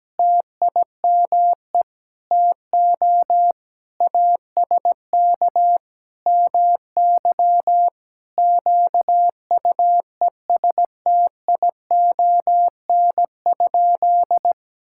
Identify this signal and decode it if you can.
Morse code, heard as a keyed tone, about 17 wpm, 700 Hz. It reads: TIME TO ASK MY QUESTION?
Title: Sentences from Top 1000 Words - Review Part 5 - 17wpm